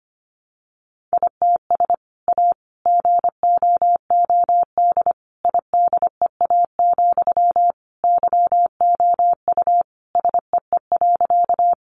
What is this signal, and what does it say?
Morse code, heard as a keyed tone, about 25 wpm, 700 Hz. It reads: ITS A GOOD IDEA, YOU SEE.
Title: Sentences from Top 600 Words 25wpm